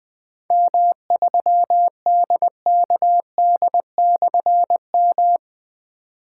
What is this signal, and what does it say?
Morse code, heard as a keyed tone, about 20 wpm, 700 Hz. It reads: M3DKD/M
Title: Difficult Call Signs 20wpm